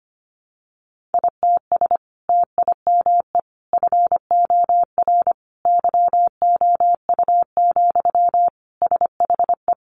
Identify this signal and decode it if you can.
Morse code, heard as a keyed tone, about 25 wpm, 700 Hz. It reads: ITS TIME FOR YOU, SHE
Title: Sentences from Top 600 Words 25wpm